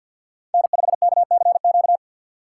Morse code, heard as a keyed tone, about 48 wpm, 700 Hz. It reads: N5KCQ
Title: USA Call Signs - 5 Characters - 6x Character Spacing - 50wpm